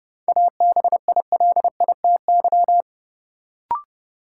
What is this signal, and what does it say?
Morse code, heard as a keyed tone, about 30 wpm, 700 Hz. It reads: ABILITY
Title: Sentences from Top 1000 Words - Review Part 3 - New Word Drill - 30wpm